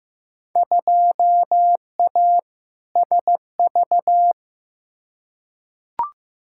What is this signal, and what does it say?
Morse code, heard as a keyed tone, about 15 wpm, 700 Hz. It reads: 2A SV
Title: Field Day Exchanges - 15wpm